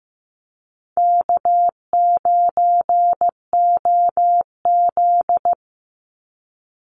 Morse code, heard as a keyed tone, about 15 wpm, 700 Hz. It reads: K9OZ
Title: NA SOTA Chaser Callsigns 15wpm